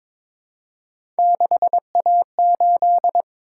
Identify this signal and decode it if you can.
Morse code, heard as a keyed tone, about 22 wpm, 700 Hz. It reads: 6A8
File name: practice / 22wpm / Three Letter-Number Combinations - 22wpm